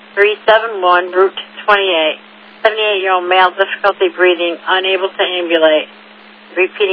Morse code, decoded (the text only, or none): none